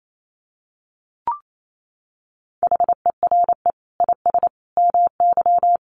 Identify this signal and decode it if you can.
Morse code, heard as a keyed tone, about 28 wpm, 700 Hz. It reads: HERE IS MY
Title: Sentences from Top 200 Words 28wpm